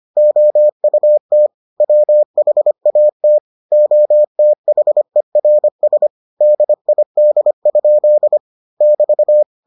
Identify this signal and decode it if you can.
Morse code, heard as a keyed tone, about 25 wpm, 600 Hz. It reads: OUT WHAT OTHERS DID? <BT>